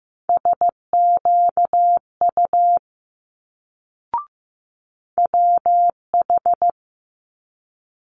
Morse code, heard as a keyed tone, about 15 wpm, 700 Hz. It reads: SQU WH
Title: Multi-letter Phonemes Quick Fire 15wpm